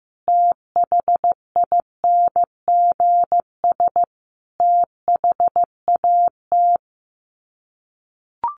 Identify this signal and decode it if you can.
Morse code, heard as a keyed tone, about 15 wpm, 700 Hz. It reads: THINGS THAT
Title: Sets of 5 Words 15wpm